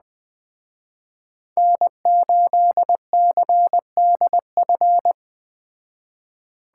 Morse code, heard as a keyed tone, about 20 wpm, 700 Hz. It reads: N8CDF
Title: Speed Racing - USA 1x3 Call Signs 17wpm